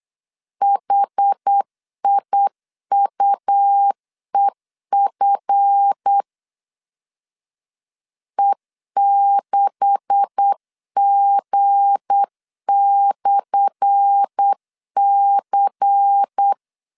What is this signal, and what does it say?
Morse code, heard as a keyed tone, about 8 wpm, 800 Hz. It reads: HIUEF E6G/C
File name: LEZIONI / 10 SIGNS / 10A SLASH 8wpm 3452